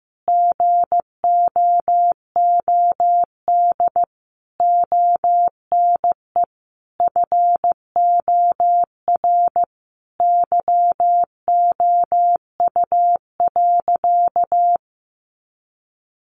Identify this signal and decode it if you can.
Morse code, heard as a keyed tone, about 15 wpm, 700 Hz. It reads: GOOD ONE FOR YOU.
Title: Sentences from Top 1500 Words 15wpm